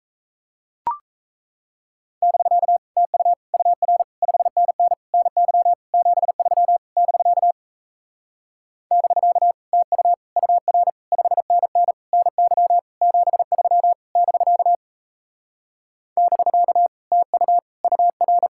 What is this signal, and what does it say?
Morse code, heard as a keyed tone, about 36 wpm, 700 Hz. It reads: <BK> TU UR 5NN NY 73 <BK> <BK> TU UR 5NN NY 73 <BK> <BK> TU UR